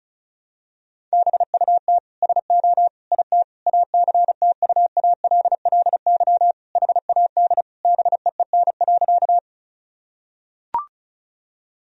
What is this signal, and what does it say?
Morse code, heard as a keyed tone, about 34 wpm, 700 Hz. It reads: BUT SO IT ACTUALLY HAD BEEN.
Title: Sentences from Top 1000 Words - Review Part 4 - 35wpm